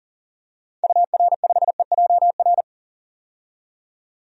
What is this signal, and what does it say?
Morse code, heard as a keyed tone, about 38 wpm, 700 Hz. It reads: UR5EJR
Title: DXCC-100 Call Signs - 6 Characters - 40wpm